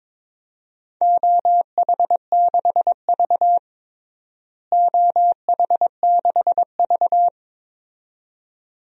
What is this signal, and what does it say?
Morse code, heard as a keyed tone, about 22 wpm, 700 Hz. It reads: OH6V OH6V
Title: DXCC-100 Call Signs - 4 Characters - Repeated 2x - 22wpm